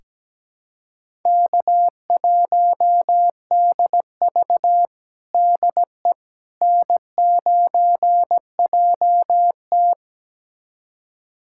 Morse code, heard as a keyed tone, about 17 wpm, 700 Hz. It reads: K1DV DE N9JT
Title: USA Callsign DE Callsign - 4 Characters - 17wpm